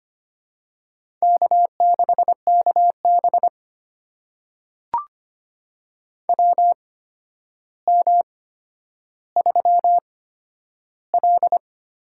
Morse code, heard as a keyed tone, about 25 wpm, spaced as for 11 wpm, 700 Hz. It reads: K6KB WM3L
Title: USA Call Signs - 4 Characters - 8x Character Spacing - 25wpm